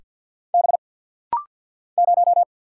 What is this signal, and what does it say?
Morse code, heard as a keyed tone, about 48 wpm, 700 Hz. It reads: B 0
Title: Single Letter-Number - Rapid-Fire 50wpm